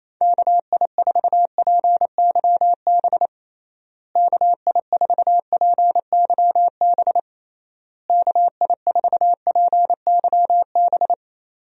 Morse code, heard as a keyed tone, about 28 wpm, 700 Hz. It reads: KI4PYB KI4PYB KI4PYB
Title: USA Call Signs - 6 Characters - Repeated 3x - 28wpm